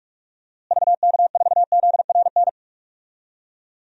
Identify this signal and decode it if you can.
Morse code, heard as a keyed tone, about 44 wpm, 700 Hz. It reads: VK4ZRN
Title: DXCC-100 Call Signs - 6 Characters - Repeated 2x - 45wpm